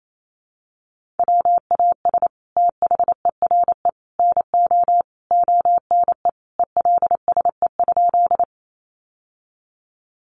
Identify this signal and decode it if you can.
Morse code, heard as a keyed tone, about 28 wpm, 700 Hz. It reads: WAS THERE NO ONE ELSE?
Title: Sentences from Top 800 Words 28wpm